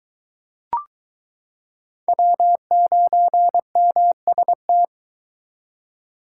Morse code, heard as a keyed tone, about 22 wpm, 700 Hz. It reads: W9MST